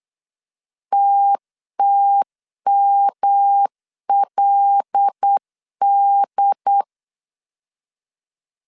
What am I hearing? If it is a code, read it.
Morse code, heard as a keyed tone, about 8 wpm, 800 Hz. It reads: TTMLD
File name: 8wpm